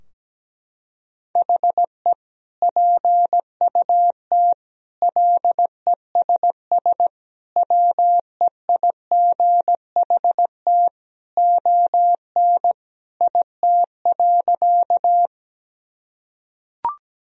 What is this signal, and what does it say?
Morse code, heard as a keyed tone, about 17 wpm, 700 Hz. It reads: HE PUT LESS WEIGHT ON IT.
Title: Sentences from Top 1000 Words - Review Part 3 - 17wpm